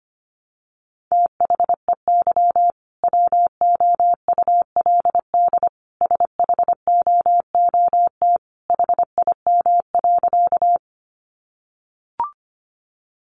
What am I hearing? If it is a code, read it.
Morse code, heard as a keyed tone, about 25 wpm, 700 Hz. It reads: THEY WOULD SHOOT HIM.
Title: Sentences from Top 1200 Words 25wpm